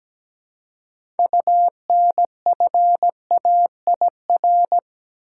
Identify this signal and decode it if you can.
Morse code, heard as a keyed tone, about 17 wpm, 700 Hz. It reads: UNFAIR
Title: Six Letter Words - 17wpm